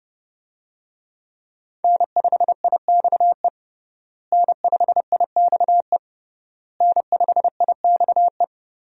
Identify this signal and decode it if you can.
Morse code, heard as a keyed tone, about 31 wpm, 700 Hz. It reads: N5IXE N5IXE N5IXE
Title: USA Call Signs - 5 Characters - Repeated 3x - 30wpm